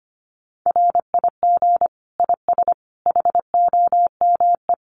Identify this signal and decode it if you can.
Morse code, heard as a keyed tone, about 25 wpm, 700 Hz. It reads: RIG IS HOME
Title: Common QSO Phrases 25wpm